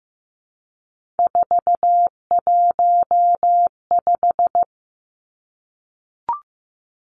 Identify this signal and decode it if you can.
Morse code, heard as a keyed tone, about 15 wpm, 700 Hz. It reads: 415
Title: Three Challenging Letter-Number Combinations 15wpm